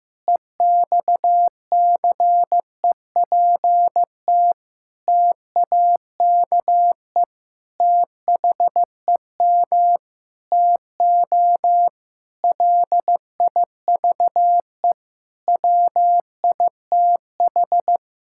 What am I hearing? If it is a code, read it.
Morse code, heard as a keyed tone, about 16 wpm, 700 Hz. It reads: EXCEPT TAKE THEM TO LIVE WITH
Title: Sentences from Top 1600 Words 15wpm